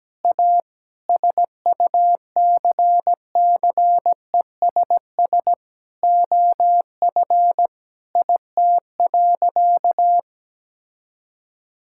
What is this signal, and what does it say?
Morse code, heard as a keyed tone, about 17 wpm, 700 Hz. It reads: A SUCCESS OF IT.
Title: Sentences from Top 1000 Words - Review Part 6 - 17wpm